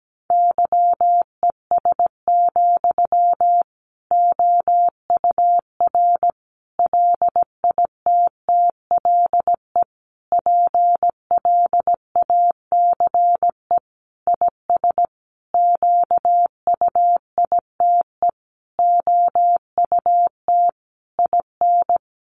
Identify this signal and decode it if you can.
Morse code, heard as a keyed tone, about 17 wpm, 700 Hz. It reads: YES, OUR LITTLE PLACE IS QUITE OUT IN
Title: Sentences from Top 700 Words 17wpm